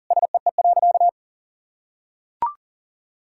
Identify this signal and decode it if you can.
Morse code, heard as a keyed tone, about 37 wpm, 700 Hz. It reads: SEE.
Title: Sentences from Top 1000 Words - Review Part 5 - 40wpm